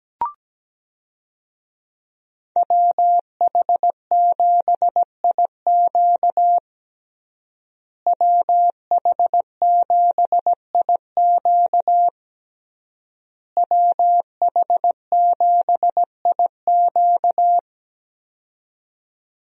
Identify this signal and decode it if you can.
Morse code, heard as a keyed tone, about 17 wpm, 700 Hz. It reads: WH7IQ WH7IQ WH7IQ
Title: USA Call Signs - 5 Characters - Repeated 3x - 17wpm